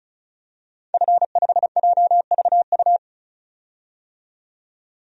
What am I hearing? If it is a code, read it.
Morse code, heard as a keyed tone, about 36 wpm, 700 Hz. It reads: F5JVU
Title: DXCC-100 Call Signs - 5 Characters - 35wpm